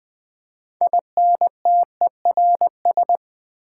Morse code, heard as a keyed tone, about 20 wpm, 700 Hz. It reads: INTERS
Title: Build A Word - 6 Letters At A Time - 12-Letter Words - 20wpm